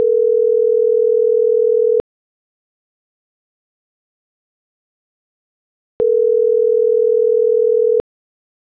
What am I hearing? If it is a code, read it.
Ringback tone (North America), 440 + 480 Hz.
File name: CENTRAL MAERSK / out andres.rdzmsk